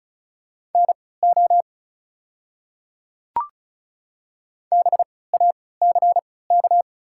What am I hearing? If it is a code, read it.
Morse code, heard as a keyed tone, about 34 wpm, spaced as for 20 wpm, 690 Hz. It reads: NO BACK
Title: 3x Character Spacing - Top 300 Words - 35wpm